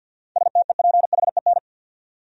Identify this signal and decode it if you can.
Morse code, heard as a keyed tone, about 47 wpm, 700 Hz. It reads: STEPHEN
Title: USA Common Names - 50wpm